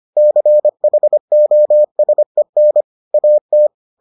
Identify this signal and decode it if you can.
Morse code, heard as a keyed tone, about 25 wpm, 600 Hz. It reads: CHOSEN AT